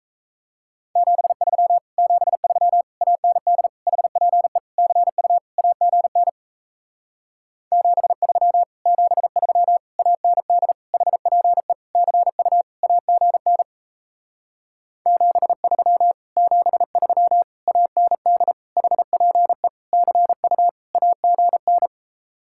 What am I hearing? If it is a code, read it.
Morse code, heard as a keyed tone, about 38 wpm, 700 Hz. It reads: 73 73 AND HPE CU AGN 73 73 AND HPE CU AGN 73 73 AND HPE CU AGN